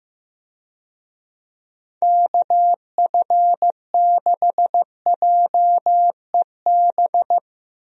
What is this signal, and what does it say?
Morse code, heard as a keyed tone, about 16 wpm, 700 Hz. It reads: KF6JEB
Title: USA Call Signs - 6 Characters - Repeated 2x - 15wpm